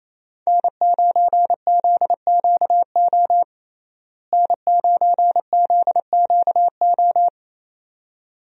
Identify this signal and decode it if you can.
Morse code, heard as a keyed tone, about 28 wpm, 700 Hz. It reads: N9ZQO N9ZQO
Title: USA Call Signs - 5 Characters - Repeated 2x - 28wpm